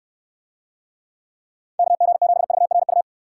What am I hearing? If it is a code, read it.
Morse code, heard as a keyed tone, about 66 wpm, 700 Hz. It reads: KG7VRU